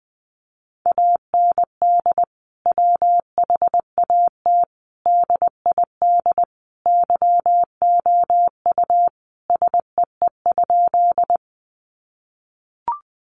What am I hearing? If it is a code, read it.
Morse code, heard as a keyed tone, about 20 wpm, 700 Hz. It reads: AND WHAT DID YOU SEE?
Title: Sentences from Top 100 Words 20wpm